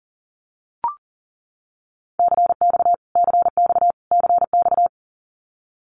Morse code, heard as a keyed tone, about 40 wpm, 700 Hz. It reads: CX CX CX
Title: Abbreviations Repeated 3x - 40wpm